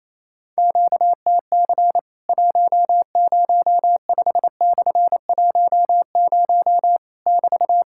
Morse code, heard as a keyed tone, about 28 wpm, 700 Hz. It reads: QTC 105/10 <BT>
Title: Worked All Europe DX Contest - QTC Exchange - 28wpm